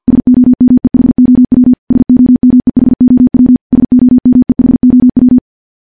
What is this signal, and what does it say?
Morse code, heard as a keyed tone, about 50 wpm, 250 Hz. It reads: SOMEHOW SOMEHOW SOMEHOW
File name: morse / morse-training-50WPM